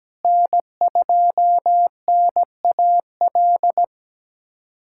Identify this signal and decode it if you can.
Morse code, heard as a keyed tone, about 17 wpm, 700 Hz. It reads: N2NAL
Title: USA Callsign DE Callsign - 5 Characters - 17wpm